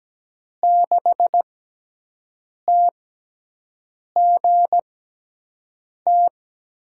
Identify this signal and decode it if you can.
Morse code, heard as a keyed tone, about 17 wpm, spaced as for 6 wpm, 700 Hz. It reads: 6TGT
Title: USA Call Signs - 6x Character Spacing - 17wpm